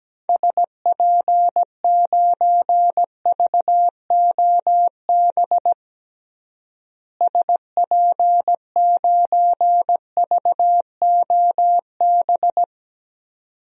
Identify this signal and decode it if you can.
Morse code, heard as a keyed tone, about 17 wpm, 700 Hz. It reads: SP9VOB SP9VOB
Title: DXCC-100 Call Signs - 6 Characters - Repeated 2x - 17wpm